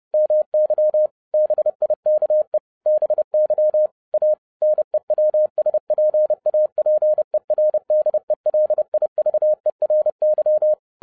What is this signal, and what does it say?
Morse code, heard as a keyed tone, about 30 wpm, 610 Hz. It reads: MY BIKE BY A NEWSPAPER DELIVERY
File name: 2011-07-mp3 / n55 BACK FROM A BIG ACCIDENT 30wpm